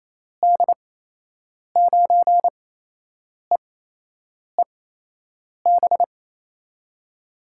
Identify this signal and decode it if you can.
Morse code, heard as a keyed tone, about 28 wpm, spaced as for 8 wpm, 700 Hz. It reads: D9EEB
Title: USA Call Signs - 8x Character Spacing - 28wpm